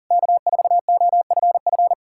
Morse code, heard as a keyed tone, about 53 wpm, 700 Hz. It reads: K4OFF